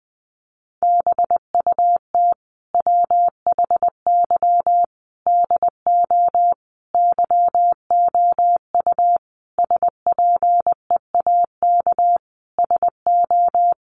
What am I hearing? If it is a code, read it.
Morse code, heard as a keyed tone, about 20 wpm, 700 Hz. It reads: BUT WHY DO YOU SPEAK SO
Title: Sentences from Top 1200 Words 20wpm